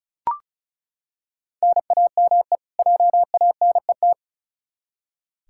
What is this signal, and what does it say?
Morse code, heard as a keyed tone, about 34 wpm, 700 Hz. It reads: NAME JANET
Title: K1USN SST QSO Practice 35wpm